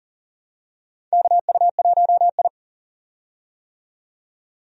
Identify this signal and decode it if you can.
Morse code, heard as a keyed tone, about 40 wpm, 700 Hz. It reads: KU1I
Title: USA Call Signs - 4 Characters - 40wpm